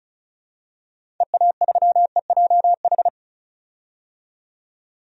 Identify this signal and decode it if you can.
Morse code, heard as a keyed tone, about 34 wpm, 700 Hz. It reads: EA3EJH
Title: DXCC-100 Call Signs - 6 Characters - 35wpm